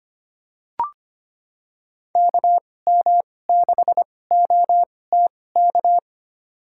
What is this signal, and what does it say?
Morse code, heard as a keyed tone, about 25 wpm, spaced as for 18 wpm, 700 Hz. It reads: KM6OTK
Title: USA Call Signs - 6 Characters - 2x Character Spacing - 25wpm